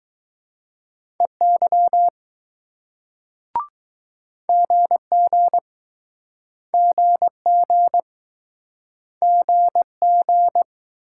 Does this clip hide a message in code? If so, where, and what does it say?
Yes: Morse code from 1.2 to 10.6 s, heard as a keyed tone, about 21 wpm, 700 Hz. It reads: EY GG GG GG